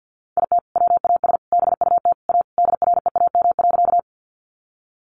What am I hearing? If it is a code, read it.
Morse code, heard as a keyed tone, about 50 wpm, 700 Hz. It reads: IT WAS BUT A DREAM.
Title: Sentences from Top 1500 Words 50wpm